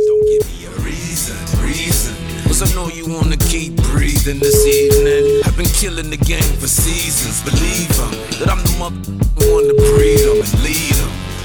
Ringback tone (Europe, 425 Hz), with noise.